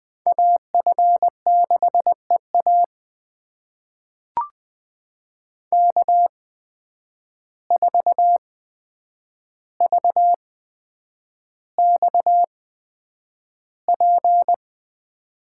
Morse code, heard as a keyed tone, about 20 wpm, spaced as for 9 wpm, 700 Hz. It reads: AF6EA K4VXP
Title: USA Call Signs - 5 Characters - 8x Character Spacing - 20wpm